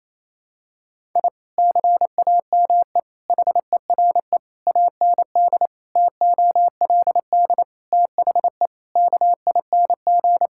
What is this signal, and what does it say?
Morse code, heard as a keyed tone, about 28 wpm, 700 Hz. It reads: I CAME HERE AND TOLD THE KING
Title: Sentences from Top 1000 Words - Review Part 6 - 28wpm